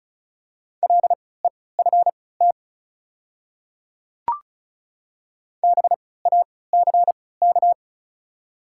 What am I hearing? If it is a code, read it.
Morse code, heard as a keyed tone, about 36 wpm, spaced as for 20 wpm, 690 Hz. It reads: LEFT BACK
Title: 3x Character Spacing - Top 500 Words - 35wpm